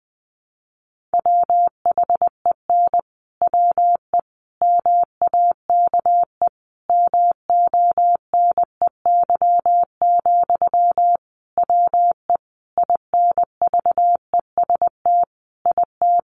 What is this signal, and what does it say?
Morse code, heard as a keyed tone, about 20 wpm, 700 Hz. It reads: WHEN WE MAKE MONEY, WE INVEST IT
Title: Sentences from Top 1100 Words 20wpm